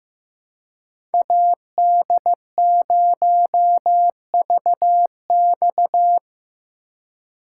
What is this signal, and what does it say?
Morse code, heard as a keyed tone, about 15 wpm, 700 Hz. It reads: AD0VX
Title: USA Call Signs - 2x Character Spacing - 15wpm